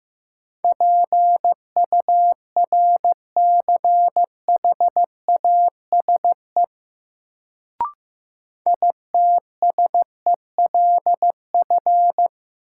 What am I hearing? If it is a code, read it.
Morse code, heard as a keyed tone, about 15 wpm, 700 Hz. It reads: PURCHASE ITSELF
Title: Sentences from Top 800 Words - New Word Drill - 15wpm